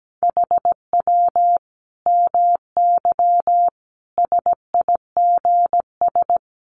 Morse code, heard as a keyed tone, about 17 wpm, 700 Hz. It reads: HW MY SIGS
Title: Common QSO Phrases 17wpm